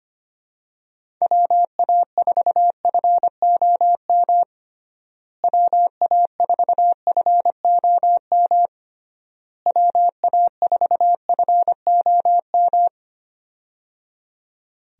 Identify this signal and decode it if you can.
Morse code, heard as a keyed tone, about 25 wpm, 700 Hz. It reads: WA4FOM WA4FOM WA4FOM
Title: USA Call Signs - 6 Characters - Repeated 3x - 25wpm